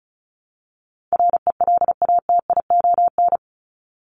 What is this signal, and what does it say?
Morse code, heard as a keyed tone, about 34 wpm, 700 Hz. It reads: RELATION
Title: ES:S Suffix - Top 100 Words with Base Words 35wpm